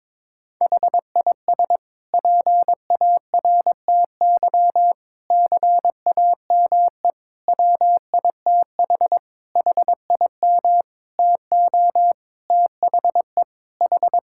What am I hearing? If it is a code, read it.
Morse code, heard as a keyed tone, about 21 wpm, 700 Hz. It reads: HIS PARTY CAME WITH HIM TO THE H
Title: Sentences from Top 1600 Words 22wpm